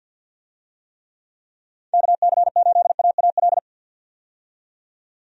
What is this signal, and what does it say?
Morse code, heard as a keyed tone, about 52 wpm, 700 Hz. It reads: KC8AAL
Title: USA Call Signs - 6 Characters - 50wpm